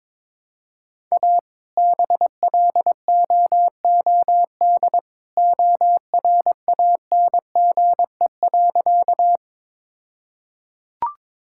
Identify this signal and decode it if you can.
Morse code, heard as a keyed tone, about 22 wpm, 700 Hz. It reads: A BLOOD ORANGE.
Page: morse-code-ninja-practice-22wpm